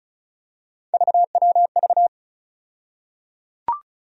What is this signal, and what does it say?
Morse code, heard as a keyed tone, about 36 wpm, 700 Hz. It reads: VWV
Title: USA Call Sign Postfixes - 3 Letters 35wpm